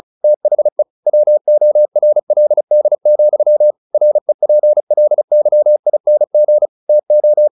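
Morse code, heard as a keyed tone, about 37 wpm, 600 Hz. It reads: THE WORLD, REPLYING TO
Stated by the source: Morse practice at 35 wpm from CW med Gnister, sending Ken Follett's novel Never in English.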